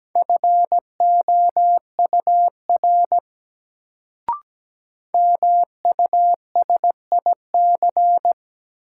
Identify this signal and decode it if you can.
Morse code, heard as a keyed tone, about 17 wpm, 700 Hz. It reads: FOUR MUSIC
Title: Sentences from Top 300 Words - New Word Drill - 17wpm